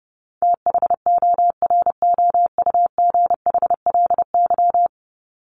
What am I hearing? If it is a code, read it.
Morse code, heard as a keyed tone, about 28 wpm, 700 Hz. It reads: THOROUGHLY
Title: LY Suffix - Top 100 Words with Base Words 30wpm